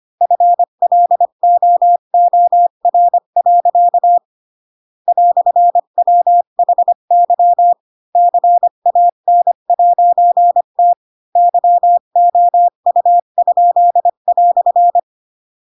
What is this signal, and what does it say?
Morse code, heard as a keyed tone, about 25 wpm, 700 Hz. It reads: FLOOR. "WHY CAN'T YOU?"
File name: Pinocchio / Pinocchio ch7 20 25 ik1hge